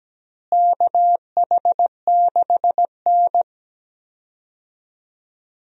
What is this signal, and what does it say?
Morse code, heard as a keyed tone, about 17 wpm, 700 Hz. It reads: KH6N
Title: DXCC-100 Call Signs - 4 Characters - Repeated 2x - 17wpm